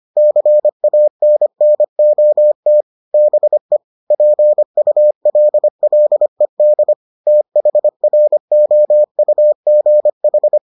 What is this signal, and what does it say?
Morse code, heard as a keyed tone, about 25 wpm, 600 Hz. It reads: CANNOT BE PULLED THROUGH